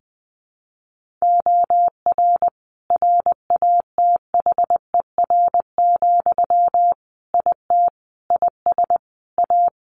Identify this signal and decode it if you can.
Morse code, heard as a keyed tone, about 19 wpm, 700 Hz. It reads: OR RATHER, IT IS A
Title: Sentences from Top 1400 Words 20wpm